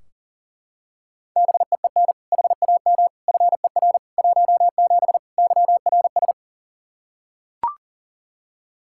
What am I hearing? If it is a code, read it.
Morse code, heard as a keyed tone, about 37 wpm, 700 Hz. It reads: BEEN HAM FER 17 YRS
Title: Practice QSO - Exchange 3 - More Info - 40wpm